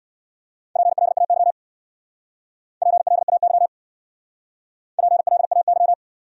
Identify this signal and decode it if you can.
Morse code, heard as a keyed tone, about 63 wpm, 700 Hz. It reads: PLAY PLAY PLAY